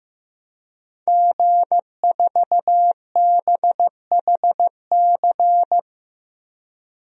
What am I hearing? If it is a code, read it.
Morse code, heard as a keyed tone, about 15 wpm, 700 Hz. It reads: G4BHC
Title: DXCC-100 Call Signs - 5 Characters - 15wpm